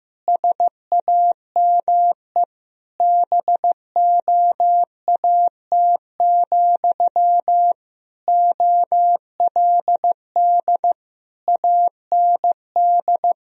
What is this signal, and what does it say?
Morse code, heard as a keyed tone, about 15 wpm, 700 Hz. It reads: SAME BOAT, OLD AND
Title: Sentences from Top 1000 Words - Review Part 3 - 15wpm